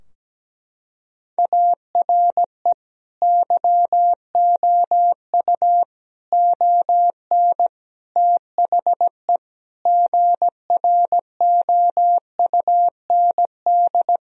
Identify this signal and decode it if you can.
Morse code, heard as a keyed tone, about 17 wpm, 700 Hz. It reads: ARE YOU ON THE GROUND
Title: Sentences from Top 1000 Words - Review Part 2 - 17wpm